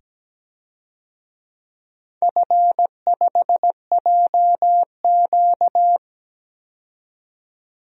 Morse code, heard as a keyed tone, about 17 wpm, 700 Hz. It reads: F5JQ
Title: DXCC-100 Call Signs - 4 Characters - 17wpm